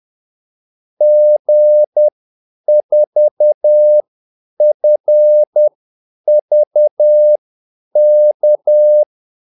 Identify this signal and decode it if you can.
Morse code, heard as a keyed tone, about 10 wpm, spaced as for 8 wpm, 600 Hz. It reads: G4FVK
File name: Callsigns-10-8wpm